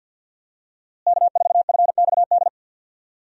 Morse code, heard as a keyed tone, about 46 wpm, 700 Hz. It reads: K4FXD
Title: USA Call Signs - 5 Characters - Repeated 3x - 50wpm